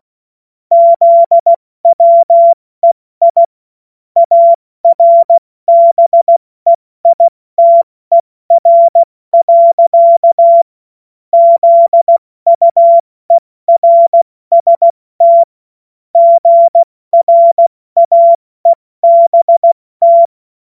Morse code, heard as a keyed tone, about 16 wpm, 680 Hz. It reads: ZWEI ARBEITER. ZUERST GRAEBT
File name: Morsekurs / Witze / Witz 22. August 2023 70 F80